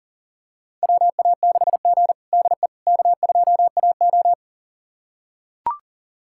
Morse code, heard as a keyed tone, about 37 wpm, 700 Hz. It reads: WA6G DE K2AO
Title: USA Callsign DE Callsign - 4 Characters - 40wpm